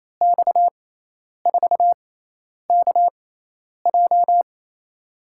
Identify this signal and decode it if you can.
Morse code, heard as a keyed tone, about 28 wpm, spaced as for 10 wpm, 700 Hz. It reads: X4KJ
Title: USA Call Signs - 5 Characters - 6x Character Spacing - 28wpm